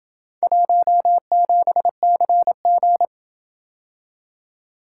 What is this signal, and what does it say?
Morse code, heard as a keyed tone, about 27 wpm, 700 Hz. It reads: 17CG